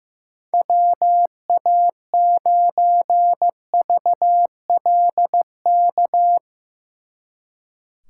Morse code, heard as a keyed tone, about 15 wpm, 700 Hz. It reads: WA9VLK
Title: Speed Racing - USA 2x3 Call Signs 15wpm